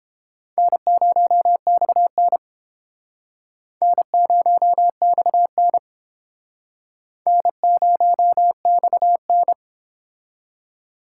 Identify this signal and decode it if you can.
Morse code, heard as a keyed tone, about 30 wpm, 700 Hz. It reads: N0XN N0XN N0XN